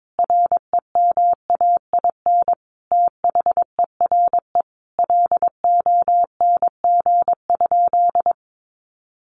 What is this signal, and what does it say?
Morse code, heard as a keyed tone, about 22 wpm, 700 Hz. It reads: REMAIN THERE LONG?
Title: Sentences from Top 1200 Words 22wpm